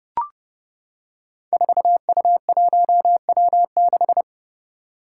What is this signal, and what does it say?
Morse code, heard as a keyed tone, about 30 wpm, 700 Hz. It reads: 4U1W6
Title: Five Challenging Letter-Number Combinations 30wpm